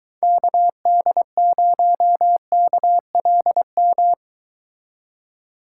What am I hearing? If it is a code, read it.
Morse code, heard as a keyed tone, about 22 wpm, 700 Hz. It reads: KD0KLM